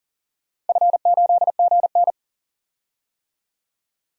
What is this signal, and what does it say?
Morse code, heard as a keyed tone, about 40 wpm, 700 Hz. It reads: F8GN
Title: DXCC-100 Call Signs - 4 Characters - 40wpm